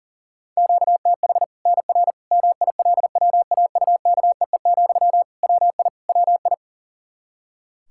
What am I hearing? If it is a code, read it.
Morse code, heard as a keyed tone, about 37 wpm, 700 Hz. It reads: QTH NR MILWAUKEE, WI WI
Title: K1USN SST QSO Practice 40wpm